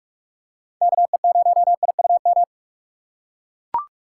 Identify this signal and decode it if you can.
Morse code, heard as a keyed tone, about 43 wpm, 700 Hz. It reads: KE0IUM
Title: USA Call Signs - 6 Characters - 6x Character Spacing - 45wpm